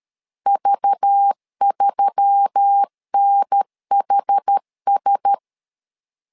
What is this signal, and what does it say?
Morse code, heard as a keyed tone, about 13 wpm, 800 Hz. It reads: V3NHS
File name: LEZIONI / 5 LBVC / 5 LBV 12wpm 1855